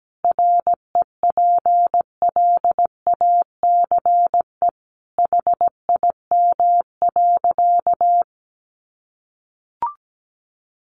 Morse code, heard as a keyed tone, about 17 wpm, 700 Hz. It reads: REPLACE HIM.
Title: Sentences from Top 1400 Words 17wpm